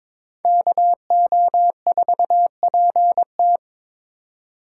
Morse code, heard as a keyed tone, about 27 wpm, 700 Hz. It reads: KO4PT
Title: Speed Racing - USA Call Signs 22wpm